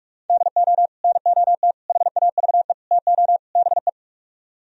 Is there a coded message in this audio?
Yes: Morse code from 0.3 to 3.9 s, heard as a keyed tone, about 48 wpm, 700 Hz. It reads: DO NOT HAVE TO BE